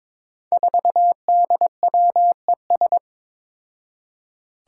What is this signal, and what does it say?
Morse code, heard as a keyed tone, about 21 wpm, 700 Hz. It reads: 4DWES
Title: Five Challenging Letter-Number Combinations 22wpm